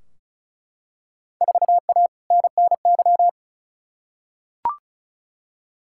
Morse code, heard as a keyed tone, about 36 wpm, 700 Hz. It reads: 4A NNY
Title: Field Day Exchanges Encore - 35wpm